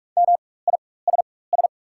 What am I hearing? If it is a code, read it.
Morse code, heard as a keyed tone, about 47 wpm, spaced as for 21 wpm, 690 Hz. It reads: MISS
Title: 4x Character Spacing - Top 500 Words - 45wpm